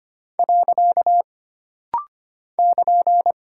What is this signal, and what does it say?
Morse code, heard as a keyed tone, about 25 wpm, 700 Hz. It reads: . <KN>
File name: Single Punctuation-Prosign - Rapid-Fire - 25wpm